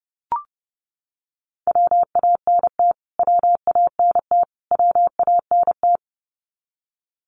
Morse code, heard as a keyed tone, about 30 wpm, 700 Hz. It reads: WANT WANT WANT
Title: Top 500 Words Repeated 30wpm